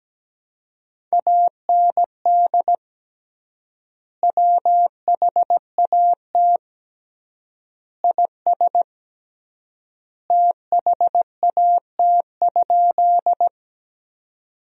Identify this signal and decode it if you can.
Morse code, heard as a keyed tone, about 17 wpm, 700 Hz. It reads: AND WHAT IS THAT?
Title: 3x Word-Spacing - Sentences from Top 100 Words - 17wpm